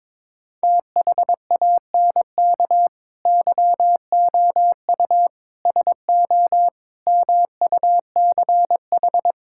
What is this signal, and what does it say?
Morse code, heard as a keyed tone, about 22 wpm, 700 Hz. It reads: THANK YOU SO MUCH
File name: practice / 22wpm / Sentences with 4-Word Phrases - 22wpm